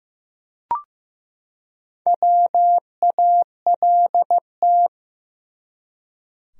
Morse code, heard as a keyed tone, about 15 wpm, 700 Hz. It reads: WALT
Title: USA Common Names - Max 5 Letters - 15wpm